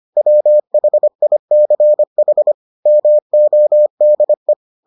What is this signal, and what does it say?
Morse code, heard as a keyed tone, about 25 wpm, 600 Hz. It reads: WHICH MODE